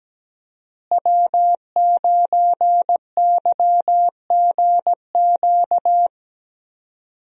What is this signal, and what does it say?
Morse code, heard as a keyed tone, about 17 wpm, 700 Hz. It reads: W9YGQ
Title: USA Call Signs - 5 Characters - Repeated 3x - 17wpm